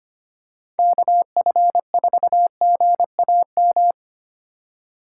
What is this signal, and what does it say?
Morse code, heard as a keyed tone, about 25 wpm, 700 Hz. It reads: KF4GAM
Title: USA Call Signs - 6 Characters - 4x Character Spacing - 25wpm